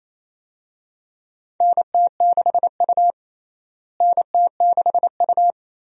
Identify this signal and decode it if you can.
Morse code, heard as a keyed tone, about 27 wpm, 700 Hz. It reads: NT6U NT6U
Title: USA Call Signs - 4 Characters - Repeated 2x - 28wpm